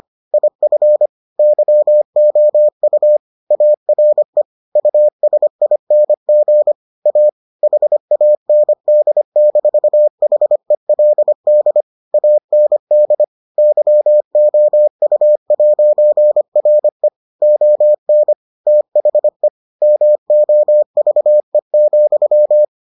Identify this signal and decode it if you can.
Morse code, heard as a keyed tone, about 25 wpm, 600 Hz. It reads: IF YOU ARE USING A HAND-HELD AND YOU'RE ON THE MOVE,